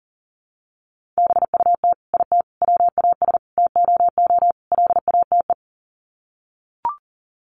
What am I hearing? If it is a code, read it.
Morse code, heard as a keyed tone, about 40 wpm, 700 Hz. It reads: BUT IT WAS TOO LATE
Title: Sets of 5 Words Encore2 40wpm